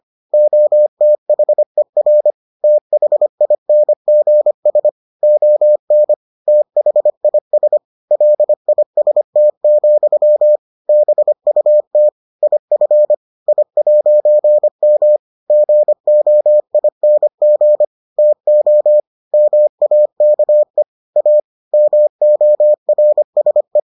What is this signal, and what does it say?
Morse code, heard as a keyed tone, about 25 wpm, 600 Hz. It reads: OTHER THINGS ON THIS LIST, BUT IF I'M GOING TO MAKE A MORSE